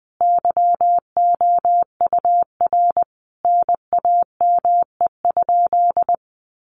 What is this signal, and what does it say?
Morse code, heard as a keyed tone, about 20 wpm, 700 Hz. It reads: YOUR NAME?
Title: Sentences from Top 1200 Words 20wpm